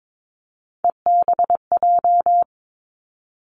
Morse code, heard as a keyed tone, about 22 wpm, 700 Hz. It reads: EBJ
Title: Three Challenging Letter-Number Combinations 22wpm